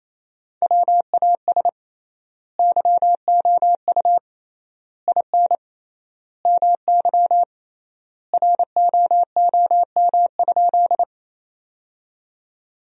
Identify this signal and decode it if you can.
Morse code, heard as a keyed tone, about 28 wpm, 700 Hz. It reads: WAS YOU IN MY ROOM?
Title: 3x Word-Spacing - Sentences from Top 300 Words - 28wpm